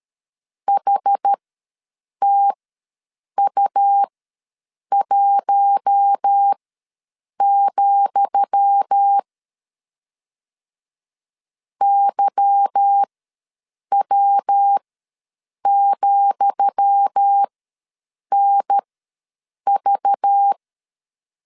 Morse code, heard as a keyed tone, about 13 wpm, spaced as for 7 wpm, 800 Hz. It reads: HTU1, YW,NV
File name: LEZIONI / 10 SIGNS / 10D COMA ws 12wpm 3618